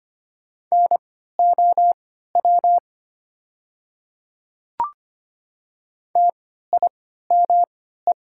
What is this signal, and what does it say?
Morse code, heard as a keyed tone, about 25 wpm, spaced as for 14 wpm, 700 Hz. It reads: NOW TIME
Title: 3x Character Spacing - Top 100 Words - 25wpm